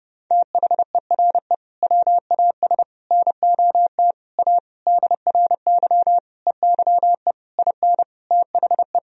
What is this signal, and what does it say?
Morse code, heard as a keyed tone, about 32 wpm, 700 Hz. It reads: THERE WAS NOT A DRY EYE IN THE
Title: Sentences from Top 1000 Words - Review Part 3 - 30wpm